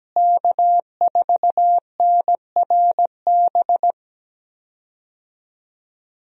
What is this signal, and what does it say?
Morse code, heard as a keyed tone, about 17 wpm, 700 Hz. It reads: K4NRB
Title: USA Call Signs - 5 Characters - Repeated 3x - 17wpm